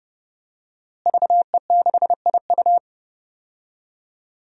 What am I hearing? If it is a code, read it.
Morse code, heard as a keyed tone, about 30 wpm, 700 Hz. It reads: VE6IU
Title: DXCC-100 Call Signs - 5 Characters - Repeated 3x - 30wpm